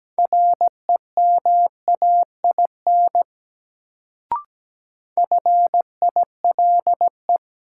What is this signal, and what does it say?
Morse code, heard as a keyed tone, about 17 wpm, 700 Hz. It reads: REMAIN FILE
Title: Sentences from Top 1200 Words - New Word Drill Alt - 17wpm